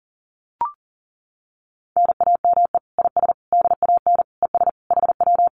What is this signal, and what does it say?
Morse code, heard as a keyed tone, about 40 wpm, 700 Hz. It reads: NAME IS DAN ES HW
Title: Common QSO Phrases Encore 40wpm